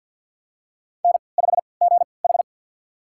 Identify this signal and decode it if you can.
Morse code, heard as a keyed tone, about 52 wpm, spaced as for 29 wpm, 700 Hz. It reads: N5GH
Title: USA Call Signs - 4 Characters - 3x Character Spacing - 50wpm